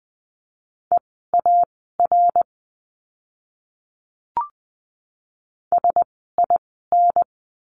Morse code, heard as a keyed tone, about 20 wpm, spaced as for 14 wpm, 700 Hz. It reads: EAR SIN
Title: 061 Lesson 11f - Letter h - Words 20wpm